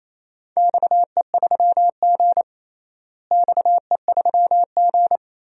Morse code, heard as a keyed tone, about 27 wpm, 700 Hz. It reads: XE3G XE3G
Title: DXCC-100 Call Signs - 4 Characters - Repeated 3x - 28wpm